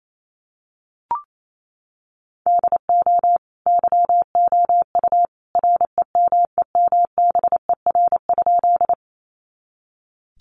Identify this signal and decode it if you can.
Morse code, heard as a keyed tone, about 28 wpm, 700 Hz. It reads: DO YOU REMEMBER?
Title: Sentences from Top 400 Words 28wpm